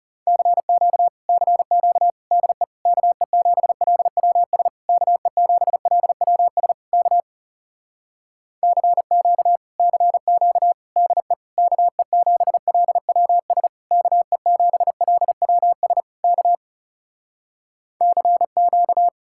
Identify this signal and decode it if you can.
Morse code, heard as a keyed tone, about 40 wpm, 700 Hz. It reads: CQ CQ DE KE7LWS KE7LWS K CQ CQ DE KE7LWS KE7LWS K CQ